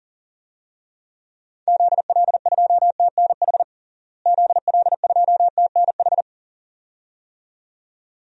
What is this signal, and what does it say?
Morse code, heard as a keyed tone, about 40 wpm, 700 Hz. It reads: ZL2TNH ZL2TNH
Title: DXCC-100 Call Signs - 6 Characters - Repeated 2x - 40wpm